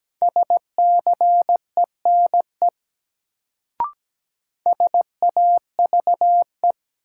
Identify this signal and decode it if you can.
Morse code, heard as a keyed tone, about 17 wpm, 700 Hz. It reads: SCENE SAVE
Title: Sentences from Top 1100 Words - New Word Drill Alt - 17wpm